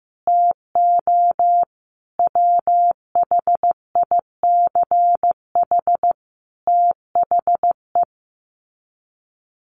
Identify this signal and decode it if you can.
Morse code, heard as a keyed tone, about 15 wpm, 700 Hz. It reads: TO WHICH THE
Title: Sets of 5 Words Encore2 15wpm